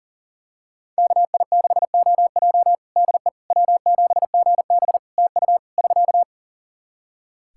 Morse code, heard as a keyed tone, about 40 wpm, 700 Hz. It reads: KI6OJ DE W7GB TU <SK>
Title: Practice QSO - Exchange 4 - Ending - 40wpm